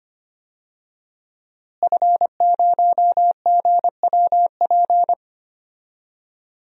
Morse code, heard as a keyed tone, about 25 wpm, 700 Hz. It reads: F0GWP
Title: DXCC-100 Call Signs - 5 Characters - Repeated 2x - 25wpm